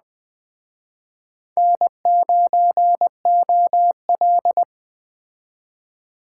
Morse code, heard as a keyed tone, about 21 wpm, 700 Hz. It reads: N9OL